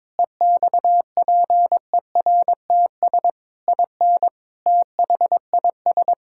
Morse code, heard as a keyed tone, about 22 wpm, 700 Hz. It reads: EXPERTS IN THIS
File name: practice / 22wpm / Sentences with 4-Word Phrases - 22wpm